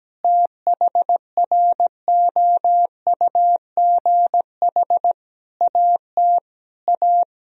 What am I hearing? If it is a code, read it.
Morse code, heard as a keyed tone, about 17 wpm, 700 Hz. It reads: THROUGH AT A
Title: Sentences from Top 1000 Words - Review Part 1 - 17wpm